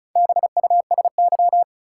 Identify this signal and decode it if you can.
Morse code, heard as a keyed tone, about 36 wpm, 700 Hz. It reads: BUSY
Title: Sentences from Top 1000 Words - Review Part 2 - New Word Drill - 35wpm